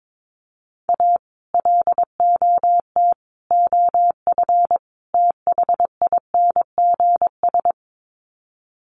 Morse code, heard as a keyed tone, about 22 wpm, 700 Hz. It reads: A LOT OF THINGS
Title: Sets of 4 Words 22wpm